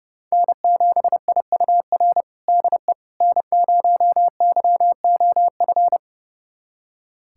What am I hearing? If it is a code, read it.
Morse code, heard as a keyed tone, about 32 wpm, 700 Hz. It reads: N7IUR DE N0YOF
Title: USA Callsign DE Callsign - 5 Characters - 30wpm